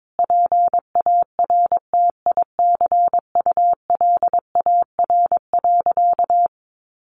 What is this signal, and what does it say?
Morse code, heard as a keyed tone, about 22 wpm, 700 Hz. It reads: PARTICULAR.
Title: Sentences from Top 600 Words 22wpm